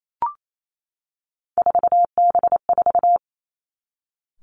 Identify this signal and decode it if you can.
Morse code, heard as a keyed tone, about 28 wpm, 700 Hz. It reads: 4B4
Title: Three Challenging Letter-Number Combinations 28wpm